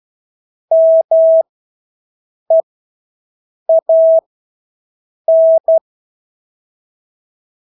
Morse code, heard as a keyed tone, about 12 wpm, spaced as for 6 wpm, 650 Hz. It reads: MEAN
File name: Engelske ord 12-6 wpm